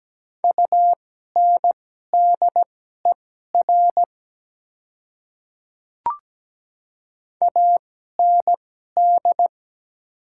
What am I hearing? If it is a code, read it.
Morse code, heard as a keyed tone, about 17 wpm, spaced as for 12 wpm, 700 Hz. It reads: UNDER AND
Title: 2x Character Spacing - Top 300 Words - 17wpm